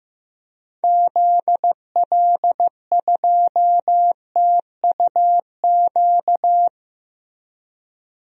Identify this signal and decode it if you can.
Morse code, heard as a keyed tone, about 15 wpm, 700 Hz. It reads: ZL2TUQ
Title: DXCC-100 Call Signs - 6 Characters - Repeated 2x - 15wpm